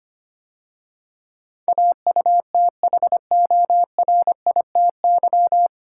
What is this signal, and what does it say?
Morse code, heard as a keyed tone, about 25 wpm, 690 Hz. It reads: AUTHORITY
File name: practice / 25wpm / 9 Letter Words - First 3 Letters Given - 25wpm